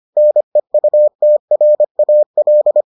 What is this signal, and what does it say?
Morse code, heard as a keyed tone, about 25 wpm, 600 Hz. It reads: NEUTRAL